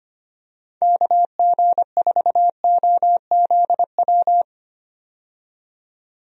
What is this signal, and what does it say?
Morse code, heard as a keyed tone, about 25 wpm, 700 Hz. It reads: KG4OZW
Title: USA Call Signs - 6 Characters - 25wpm